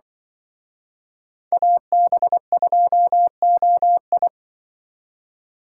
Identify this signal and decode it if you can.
Morse code, heard as a keyed tone, about 23 wpm, 700 Hz. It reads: AB2OI
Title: Speed Racing - USA 2x2 Call Signs 20wpm